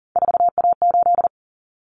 Morse code, heard as a keyed tone, about 40 wpm, 700 Hz. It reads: 4A8
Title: Common Cut Numbers - 3-Digit 40wpm